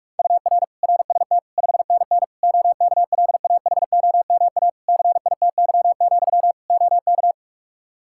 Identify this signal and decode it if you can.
Morse code, heard as a keyed tone, about 46 wpm, 700 Hz. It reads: UR RST 5NN OKLAHOMA CITY, OK
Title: Practice QSO - Exchange 1 - Beginning - 45wpm